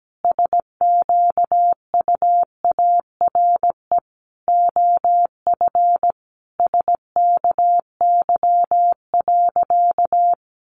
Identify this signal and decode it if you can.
Morse code, heard as a keyed tone, about 17 wpm, 700 Hz. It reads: SQUARE OF SKY.
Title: Sentences from Top 1200 Words 17wpm